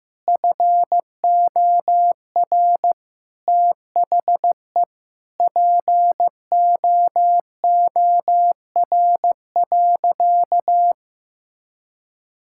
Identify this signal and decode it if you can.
Morse code, heard as a keyed tone, about 15 wpm, 700 Hz. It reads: FOR THE POOR.
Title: Sentences from Top 1000 Words - Review Part 5 - 15wpm